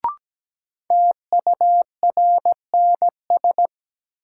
Morse code, heard as a keyed tone, about 17 wpm, 700 Hz. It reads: TURNS
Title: Sets of 5 Words Encore - New Word Drill - 17wpm